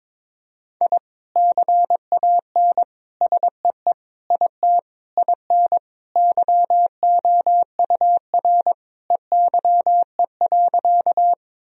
Morse code, heard as a keyed tone, about 21 wpm, 700 Hz. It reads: I CAN SEE IT IN YOUR EYE.
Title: Sentences from Top 1000 Words - Review Part 1 - 22wpm